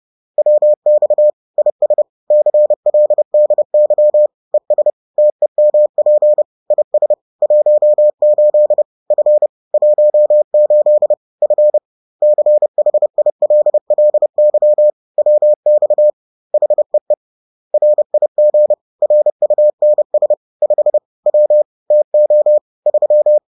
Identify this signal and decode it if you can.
Morse code, heard as a keyed tone, about 30 wpm, 600 Hz. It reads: WX IS CLDY ES TEMP IS 18 F 18 F CHILLY WX HEE RIG RUNS 5 W TO 3
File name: 30wpm-4